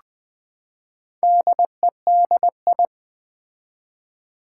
Morse code, heard as a keyed tone, about 20 wpm, 700 Hz. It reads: DEDI
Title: Build A Word - 4 Letters At A Time - 10-Letter Words - 20wpm